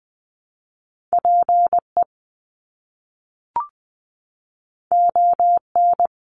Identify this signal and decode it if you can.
Morse code, heard as a keyed tone, about 20 wpm, 700 Hz. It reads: PE ON
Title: Canadian Provinces and Territories 20wpm